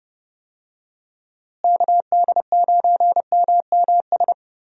Morse code, heard as a keyed tone, about 30 wpm, 700 Hz. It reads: KD9MMS
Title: USA Call Signs - 6 Characters - Repeated 3x - 30wpm